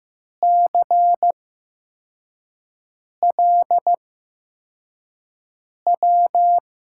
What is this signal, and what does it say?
Morse code, heard as a keyed tone, about 15 wpm, spaced as for 4 wpm, 700 Hz. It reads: CLW
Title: USA Call Signs - 6 Characters - 8x Character Spacing - 15wpm